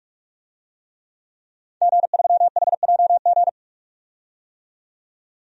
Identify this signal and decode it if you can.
Morse code, heard as a keyed tone, about 41 wpm, 700 Hz. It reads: G3HJG
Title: DXCC-100 Call Signs - 5 Characters - 45wpm